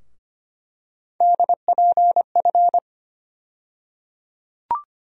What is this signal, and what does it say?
Morse code, heard as a keyed tone, about 25 wpm, 700 Hz. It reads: DPF
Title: USA Call Sign Postfixes - 3 Letters 25wpm